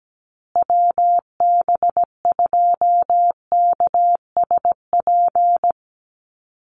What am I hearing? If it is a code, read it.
Morse code, heard as a keyed tone, about 17 wpm, 700 Hz. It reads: WB2KSP
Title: USA Call Signs Repeated 3x 17wpm